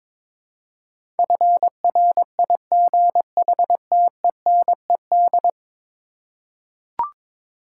Morse code, heard as a keyed tone, about 22 wpm, 700 Hz. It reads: FRIGHTENED
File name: practice / 22wpm / 10 Letter Words - First 3 Letters Given - 22wpm